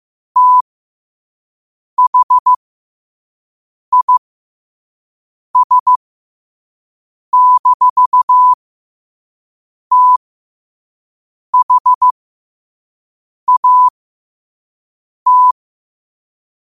Morse code, heard as a keyed tone, about 15 wpm, spaced as for 5 wpm, 1000 Hz. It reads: THIS-THAT
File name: qotdfiles / 5wpm